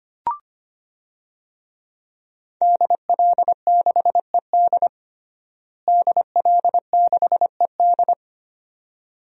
Morse code, heard as a keyed tone, about 25 wpm, 700 Hz. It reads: DL6ED DL6ED
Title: DXCC-100 Call Signs - 5 Characters - Repeated 2x - 25wpm